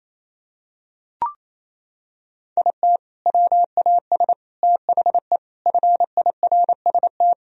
Morse code, heard as a keyed tone, about 28 wpm, 700 Hz. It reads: IT WAS THE FIRST
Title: Sentences from Top 1000 Words - Review Part 2 - 28wpm